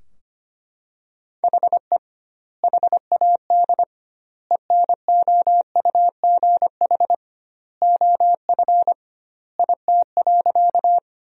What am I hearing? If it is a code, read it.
Morse code, heard as a keyed tone, about 25 wpm, 700 Hz. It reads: HE HAD ENOUGH OF IT.
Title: 2x Word-Spacing - Sentences from Top 400 Words - 25wpm